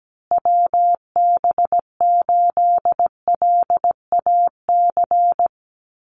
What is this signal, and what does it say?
Morse code, heard as a keyed tone, about 17 wpm, 700 Hz. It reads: WB8LAC
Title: USA Call Signs Repeated 2x 17wpm